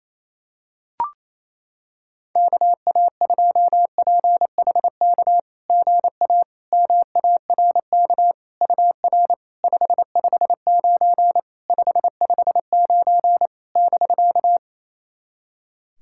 Morse code, heard as a keyed tone, about 27 wpm, 690 Hz. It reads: KA2PHK GA MARK UR 559 559 <BK>
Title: SOTA QSO Practice 28wpm